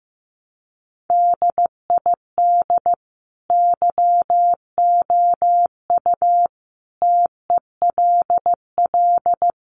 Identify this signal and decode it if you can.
Morse code, heard as a keyed tone, about 15 wpm, 700 Hz. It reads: DID YOU TELL
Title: Sentences from Top 200 Words 15wpm